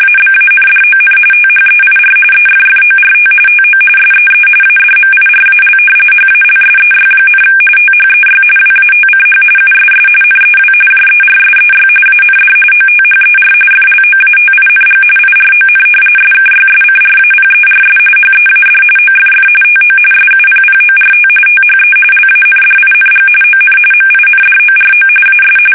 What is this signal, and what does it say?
RTTY, about 45.45 baud. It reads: ABANDON THEIR EFFORT TO SEE ADMIRAL HALSEY, THEY WATCH USS NEW JERSEY AND OTHER FLOATING IRON SKYSCRAPERS HEAD OFF TO SUPPORT THE LANDING ON MINDORO AS
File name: fsk850 / battleshipnj